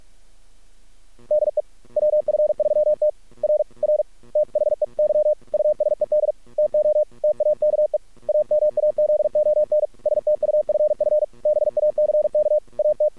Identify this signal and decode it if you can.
Morse code, heard as a keyed tone, about 48 wpm, 610 Hz. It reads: DE WF3T R R THEY USED TO TAKE AMAZON STUFF BACK AT
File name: archived / CRSnet